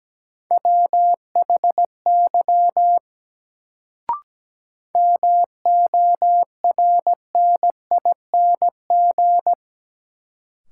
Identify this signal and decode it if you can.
Morse code, heard as a keyed tone, about 17 wpm, 700 Hz. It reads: WHY MORNING
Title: Sets of 3 Words - New Word Drill - 17wpm